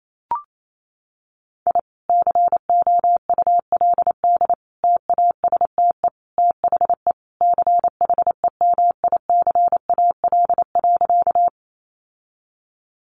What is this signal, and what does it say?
Morse code, heard as a keyed tone, about 28 wpm, 700 Hz. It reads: I COULD TASTE THE CHEMICAL.
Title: Sentences from Top 1200 Words 28wpm